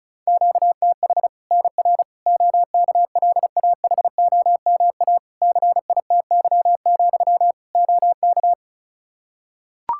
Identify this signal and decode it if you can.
Morse code, heard as a keyed tone, about 36 wpm, 700 Hz. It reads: QTH NR OKLAHOMA CITY, OK
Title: Practice QSO - Exchange 1 - Beginning - 35wpm